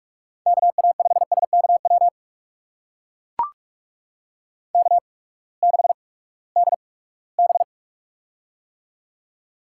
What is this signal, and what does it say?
Morse code, heard as a keyed tone, about 44 wpm, spaced as for 22 wpm, 700 Hz. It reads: KA5SKW K6DB
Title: USA Call Signs - 8x Character Spacing - 45wpm